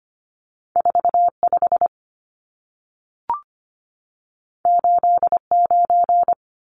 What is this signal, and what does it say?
Morse code, heard as a keyed tone, about 25 wpm, 700 Hz. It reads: 45 89
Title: Numbers - 2-Digit 25wpm